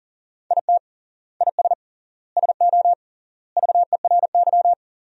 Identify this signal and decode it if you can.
Morse code, heard as a keyed tone, about 37 wpm, 700 Hz. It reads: IT IS SO VERY
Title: 3x Word-Spacing - Sentences from Top 300 Words - 40wpm